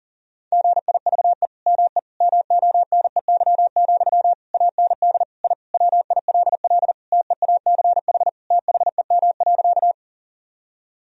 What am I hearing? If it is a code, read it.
Morse code, heard as a keyed tone, about 40 wpm, 700 Hz. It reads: GIVE ME MONEY, AND I WILL TEACH THEM.
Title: Sentences from Top 1000 Words - Review Part 2 - 40wpm